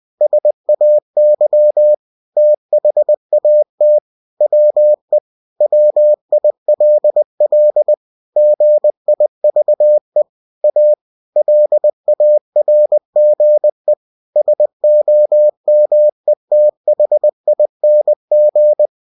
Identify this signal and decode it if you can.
Morse code, heard as a keyed tone, about 20 wpm, 600 Hz. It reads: SAY THAT WE WILL GIVE A LARGE SOMETHING
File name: cw-books / winnie-the-pooh / winnie-20wpm-page-0030